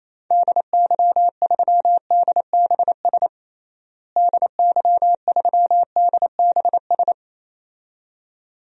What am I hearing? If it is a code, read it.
Morse code, heard as a keyed tone, about 28 wpm, 700 Hz. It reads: DY3DBS DY3DBS
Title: DXCC-100 Call Signs - 6 Characters - Repeated 2x - 28wpm